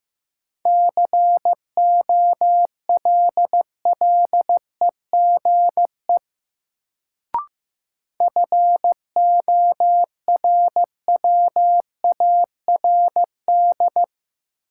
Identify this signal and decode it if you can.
Morse code, heard as a keyed tone, about 15 wpm, 700 Hz. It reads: COLLEGE FORWARD
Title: Sentences from Top 1300 Words - New Word Drill - 15wpm